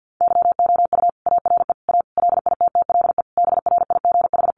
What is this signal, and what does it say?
Morse code, heard as a keyed tone, about 50 wpm, 700 Hz. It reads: YOU ARE A LITTLE BRIGH
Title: Sentences from Top 1000 Words 50wpm